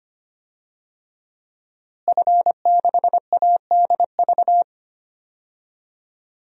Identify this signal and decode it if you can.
Morse code, heard as a keyed tone, about 25 wpm, 700 Hz. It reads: F6ADV
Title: DXCC-100 Call Signs - 5 Characters - Repeated 3x - 25wpm